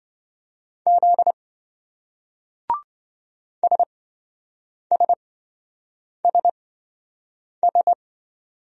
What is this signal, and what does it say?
Morse code, heard as a keyed tone, about 27 wpm, spaced as for 7 wpm, 700 Hz. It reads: Z SSSS